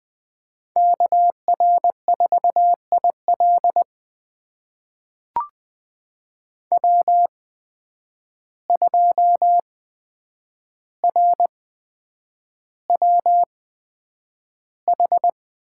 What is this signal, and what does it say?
Morse code, heard as a keyed tone, about 20 wpm, spaced as for 9 wpm, 700 Hz. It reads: KR4IL W2RWH
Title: USA Call Signs - 5 Characters - 8x Character Spacing - 20wpm